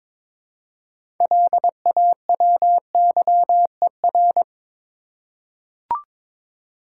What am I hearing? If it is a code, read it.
Morse code, heard as a keyed tone, about 22 wpm, 700 Hz. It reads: LAWYER
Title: Six Letter Words - 22wpm